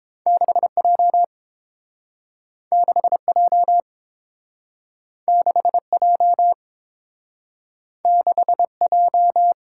Morse code, heard as a keyed tone, about 30 wpm, 700 Hz. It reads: 6J 6J 6J 6J